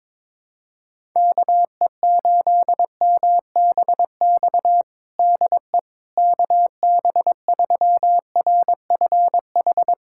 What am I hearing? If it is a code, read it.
Morse code, heard as a keyed tone, about 22 wpm, 700 Hz. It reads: KE8MBX DE KB3RFH
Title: USA Callsign DE Callsign 22wpm